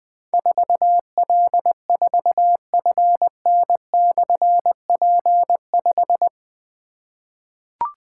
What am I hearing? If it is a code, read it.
Morse code, heard as a keyed tone, about 21 wpm, 700 Hz. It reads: 4L4FN/P5
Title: Difficult Call Signs 20wpm